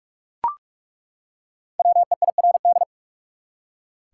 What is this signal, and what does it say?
Morse code, heard as a keyed tone, about 43 wpm, 700 Hz. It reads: WEIRD
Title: Five Letter Words - 45wpm